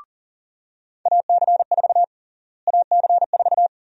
Morse code, heard as a keyed tone, about 40 wpm, 700 Hz. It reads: AC4 AC4
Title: USA Call Sign Prefixes - Repeated 2x - 40wpm